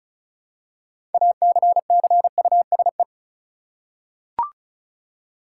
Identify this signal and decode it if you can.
Morse code, heard as a keyed tone, about 34 wpm, 700 Hz. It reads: ACCUSE
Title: Six Letter Words - 35wpm